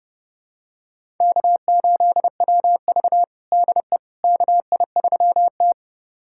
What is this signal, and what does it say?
Morse code, heard as a keyed tone, about 30 wpm, 700 Hz. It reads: K8WV DE KI3T
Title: USA Callsign DE Callsign - 4 Characters - 30wpm